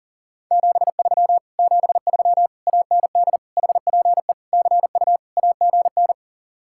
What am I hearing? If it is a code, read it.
Morse code, heard as a keyed tone, about 47 wpm, 700 Hz. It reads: 73 73 AND HPE CU AGN
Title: Speed-Racing Common QSO Phrases - 40wpm